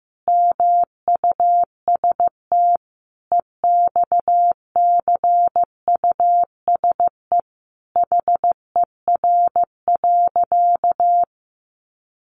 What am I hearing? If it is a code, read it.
Morse code, heard as a keyed tone, about 15 wpm, 700 Hz. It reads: MUST EXCUSE HER.
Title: Sentences from Top 700 Words 15wpm